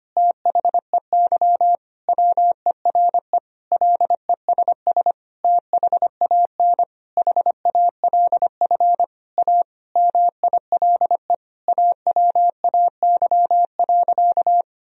Morse code, heard as a keyed tone, about 25 wpm, 700 Hz. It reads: THEY WERE LESS THAN HALF A MILE AWAY.
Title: Sentences from Top 1000 Words - Review Part 2 - 25wpm